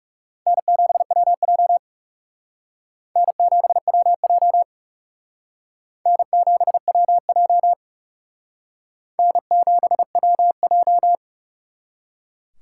Morse code, heard as a keyed tone, about 37 wpm, 700 Hz. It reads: N7WJ N7WJ N7WJ N7WJ
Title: Speed Racing - USA 1x2 Call Signs 30wpm